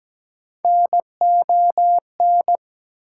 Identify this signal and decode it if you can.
Morse code, heard as a keyed tone, about 17 wpm, 700 Hz. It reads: NON
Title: Prefixes and Suffixes 17wpm